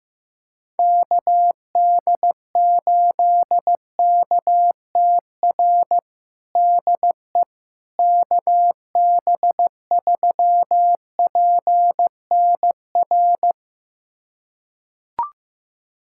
Morse code, heard as a keyed tone, about 16 wpm, 700 Hz. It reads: KD8KTR DE KB3PNR
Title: USA Callsign DE Callsign - 6 Characters - 15wpm